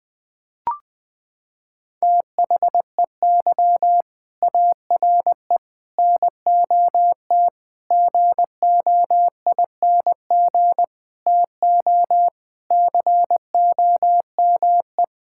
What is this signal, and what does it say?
Morse code, heard as a keyed tone, about 21 wpm, 700 Hz. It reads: THEY ARE NOT GOING TO COME
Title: Sentences with 4-Word Phrases - 20wpm